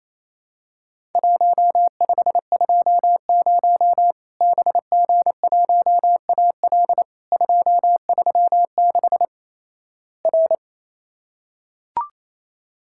Morse code, heard as a keyed tone, about 28 wpm, 700 Hz. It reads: 1520 BG1AL 236 R
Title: Worked All Europe DX Contest - QTC Exchange - 28wpm